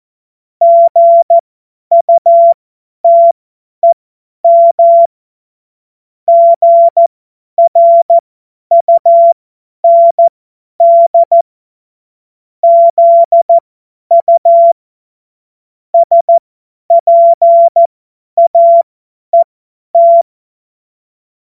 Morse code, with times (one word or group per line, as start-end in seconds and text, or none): 0.6-5.1 GUTEM
6.3-11.4 GRUND
12.6-14.7 ZU
15.9-20.2 SPAET